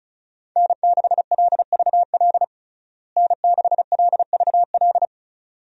Morse code, heard as a keyed tone, about 36 wpm, 700 Hz. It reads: N6LVL N6LVL
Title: USA Call Signs - 5 Characters - Repeated 2x - 35wpm